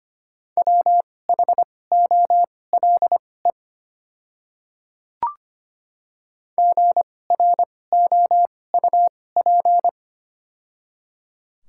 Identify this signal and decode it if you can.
Morse code, heard as a keyed tone, about 25 wpm, spaced as for 18 wpm, 700 Hz. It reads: WHOLE GROUP
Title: 2x Character Spacing - Top 500 Words - 25wpm